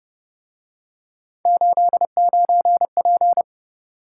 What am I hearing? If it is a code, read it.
Morse code, heard as a keyed tone, about 31 wpm, 700 Hz. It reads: 89P
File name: Speed-Racing - Three Letter-Number Combinations - 25wpm